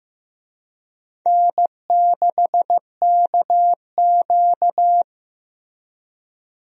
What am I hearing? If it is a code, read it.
Morse code, heard as a keyed tone, about 15 wpm, 700 Hz. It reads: N6KQ
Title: USA Call Signs - 4 Characters - 15wpm